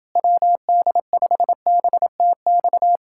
Morse code, heard as a keyed tone, about 29 wpm, 700 Hz. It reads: WD5BTX